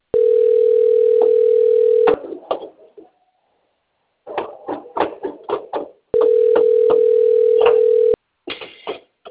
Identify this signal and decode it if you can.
Ringback tone (North America), 440 + 480 Hz.
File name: out- - - - - .71463